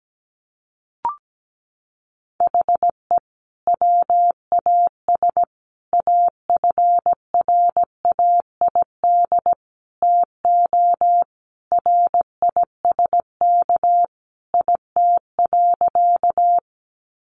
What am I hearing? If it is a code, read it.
Morse code, heard as a keyed tone, about 17 wpm, 700 Hz. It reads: HE WAS AFRAID TO RISK IT.
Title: Sentences from Top 1500 Words 17wpm